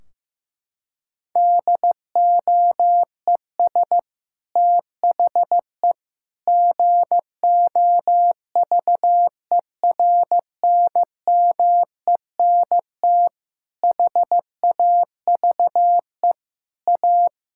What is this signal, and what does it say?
Morse code, heard as a keyed tone, about 16 wpm, 700 Hz. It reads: DOES THE GOVERNMENT HAVE A
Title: Sentences from Top 1600 Words 15wpm